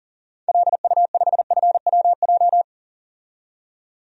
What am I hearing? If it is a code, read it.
Morse code, heard as a keyed tone, about 40 wpm, 700 Hz. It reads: LU5FWJ
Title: DXCC-100 Call Signs - 6 Characters - 40wpm